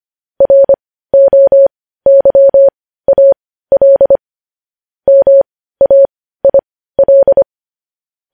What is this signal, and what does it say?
Morse code, heard as a keyed tone, about 25 wpm, spaced as for 15 wpm, 550 Hz. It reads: ROYAL MAIL
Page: News Headlines in Morse Code at 15 WPM